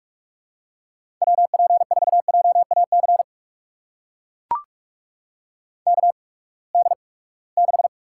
Morse code, heard as a keyed tone, about 43 wpm, spaced as for 25 wpm, 700 Hz. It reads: WP4JAC KD6
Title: USA Call Signs - 6 Characters - 8x Character Spacing - 45wpm